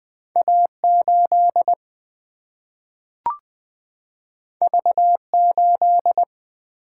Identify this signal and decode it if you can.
Morse code, heard as a keyed tone, about 20 wpm, 700 Hz. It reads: A8 V8
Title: Single-Letter Single-Number 20wpm